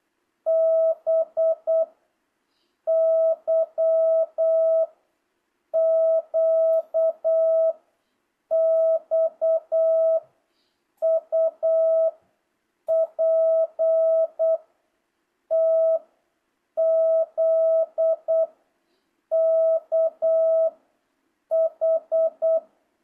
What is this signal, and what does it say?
Morse code, heard as a keyed tone, about 8 wpm, 660 Hz. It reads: B Y Q X U P T Z K H